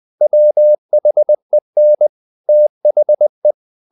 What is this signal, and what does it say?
Morse code, heard as a keyed tone, about 20 wpm, 600 Hz. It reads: WHEN THE
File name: the old man and the sea part 2 20wpm